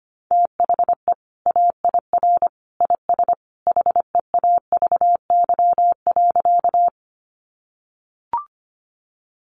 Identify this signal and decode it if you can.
Morse code, heard as a keyed tone, about 25 wpm, 700 Hz. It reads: THE AIR IS HEAVY.
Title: Sentences from Top 1100 Words 25wpm